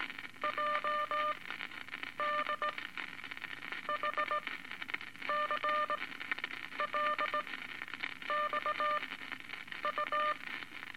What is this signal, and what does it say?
Morse code, heard as a keyed tone, about 18 wpm, spaced as for 8 wpm, 1200 Hz, with noise. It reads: JDHCL XU
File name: Eight Words Per Minute